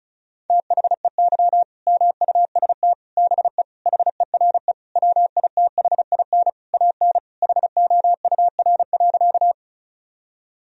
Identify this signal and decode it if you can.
Morse code, heard as a keyed tone, about 34 wpm, 700 Hz. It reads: THEY MUST BE HERE WITHIN AN HOUR.
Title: Sentences from Top 1000 Words - Review Part 4 - 35wpm